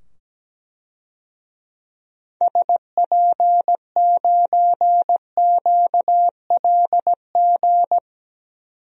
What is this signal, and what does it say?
Morse code, heard as a keyed tone, about 17 wpm, 700 Hz. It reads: SP9QLG
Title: DXCC-100 Call Signs - 6 Characters - 17wpm